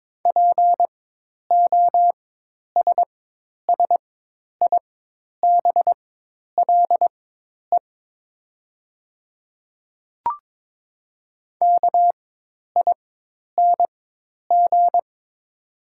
Morse code, heard as a keyed tone, about 22 wpm, spaced as for 10 wpm, 690 Hz. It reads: POSSIBLE KING
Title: 4x Character Spacing - Top 500 Words - 22wpm